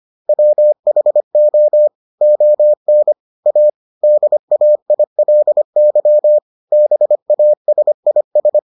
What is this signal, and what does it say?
Morse code, heard as a keyed tone, about 25 wpm, 600 Hz. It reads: WHO ON A DAILY BASIS